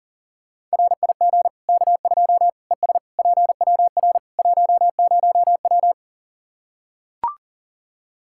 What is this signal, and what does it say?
Morse code, heard as a keyed tone, about 37 wpm, 700 Hz. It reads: RIG K2 ES PWR 10W
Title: Practice QSO - Exchange 2 - Info - 40wpm